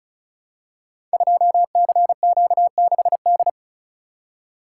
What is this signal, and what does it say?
Morse code, heard as a keyed tone, about 36 wpm, 700 Hz. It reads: 2CQ6D
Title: Code Groups - 5 Letter:Numbers - 35wpm